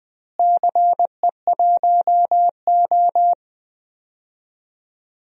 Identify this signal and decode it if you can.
Morse code, heard as a keyed tone, about 20 wpm, 700 Hz. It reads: CE1O
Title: DXCC-100 Call Signs - 4 Characters - Repeated 3x - 20wpm